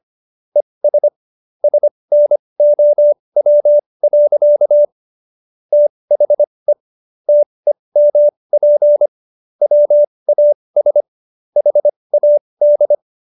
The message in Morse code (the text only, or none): ES SNOW. THE TEMP WAS HAD